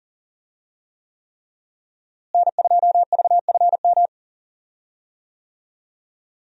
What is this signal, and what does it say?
Morse code, heard as a keyed tone, about 38 wpm, 700 Hz. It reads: N2VFM
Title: USA Call Signs - 5 Characters - Repeated 3x - 40wpm